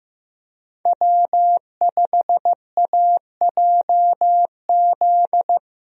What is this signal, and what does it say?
Morse code, heard as a keyed tone, about 15 wpm, 700 Hz. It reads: W5AJZ
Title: USA Call Signs - 5 Characters - Repeated 3x - 15wpm